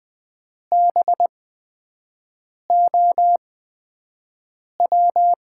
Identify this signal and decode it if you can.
Morse code, heard as a keyed tone, about 20 wpm, spaced as for 5 wpm, 700 Hz. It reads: BOW
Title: USA Call Signs - 6 Characters - 8x Character Spacing - 20wpm